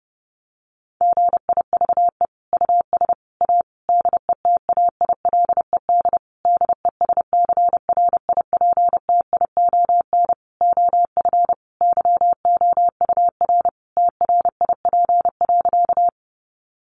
Morse code, heard as a keyed tone, about 28 wpm, 700 Hz. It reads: GIVE US A DETAILED DESCRIPTION OF YOUR TRIP.
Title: Sentences from Top 1500 Words 30wpm